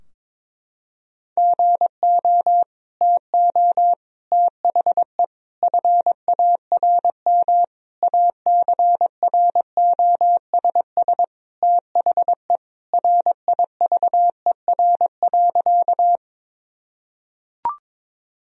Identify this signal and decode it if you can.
Morse code, heard as a keyed tone, about 21 wpm, 700 Hz. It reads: GO TO THE FARM ACROSS THE RIVER.
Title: Sentences from Top 1000 Words - Review Part 2 - 22wpm